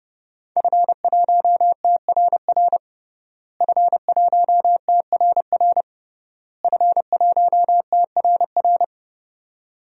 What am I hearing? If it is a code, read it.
Morse code, heard as a keyed tone, about 30 wpm, 700 Hz. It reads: F1TRR F1TRR F1TRR
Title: DXCC-100 Call Signs - 5 Characters - Repeated 3x - 30wpm